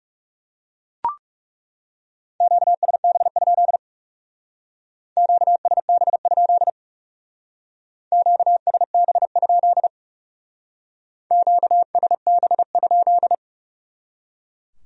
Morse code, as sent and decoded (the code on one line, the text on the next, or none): --.- ... -... ..--.. / --.- ... -... ..--.. / --.- ... -... ..--.. / --.- ... -... ..--..
QSB? QSB? QSB? QSB?